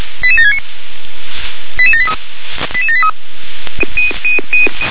Five-tone selective call (ZVEI), heard as quick runs of five tones, tones of about 70 ms. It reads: 88769 88783 88783
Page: /357/SDR/ZVEI/88773/2024.03.23/